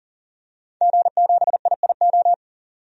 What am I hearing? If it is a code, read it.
Morse code, heard as a keyed tone, about 37 wpm, 700 Hz. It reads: G7IIO
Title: DXCC-100 Call Signs - 5 Characters - 40wpm